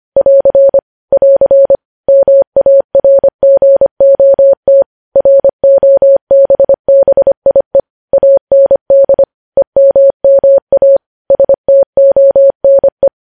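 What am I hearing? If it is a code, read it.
Morse code, heard as a keyed tone, about 25 wpm, 550 Hz. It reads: <AR> <AR> MARGOT ROBBIE AND EMMA STONE
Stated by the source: Morse code news headlines sent at 25.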